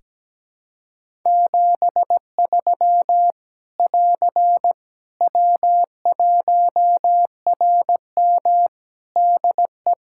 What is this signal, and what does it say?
Morse code, heard as a keyed tone, about 17 wpm, 700 Hz. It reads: 73 <AR> W1RM DE
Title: Practice QSO - Exchange 4 - Ending - 17wpm